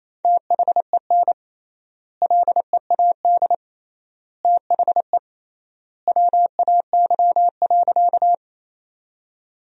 Morse code, heard as a keyed tone, about 28 wpm, 700 Hz. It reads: THEN LEAD THE WAY.
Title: 3x Word-Spacing - Sentences from Top 500 Words - 28wpm